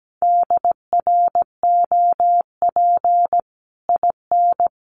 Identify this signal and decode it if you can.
Morse code, heard as a keyed tone, about 17 wpm, 700 Hz. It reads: DROP IN
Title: Sentences from Top 1400 Words 17wpm